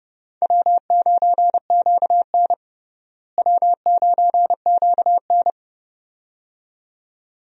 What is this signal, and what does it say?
Morse code, heard as a keyed tone, about 30 wpm, 700 Hz. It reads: W9QN W9QN
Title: USA Call Signs - 4 Characters - Repeated 2x - 30wpm